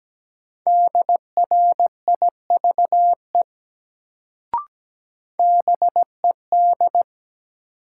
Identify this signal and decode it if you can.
Morse code, heard as a keyed tone, about 17 wpm, 700 Hz. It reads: DRIVE BED
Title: Sentences from Top 500 Words - New Word Drill - 17wpm